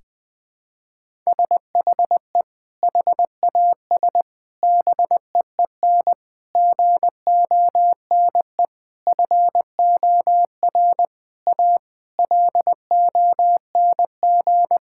Morse code, heard as a keyed tone, about 20 wpm, 700 Hz. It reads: SHE HAS BEEN GONE FOR A LONG
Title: Sentences with 3-Word Phrases - 20wpm